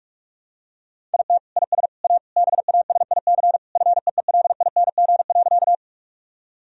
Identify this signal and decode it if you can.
Morse code, heard as a keyed tone, about 46 wpm, 700 Hz. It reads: IT IS A BASIC FEELING.
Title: Sentences from Top 1000 Words - Review Part 3 - 45wpm